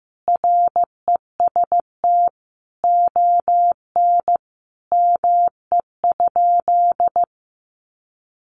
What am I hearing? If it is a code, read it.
Morse code, heard as a keyed tone, about 14 wpm, 700 Hz. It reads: REST ON ME?
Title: Sentences from Top 500 Words 15wpm